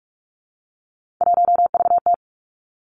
Morse code, heard as a keyed tone, about 46 wpm, 700 Hz. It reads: 1VT
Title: Three Challenging Letter-Number Combinations 45wpm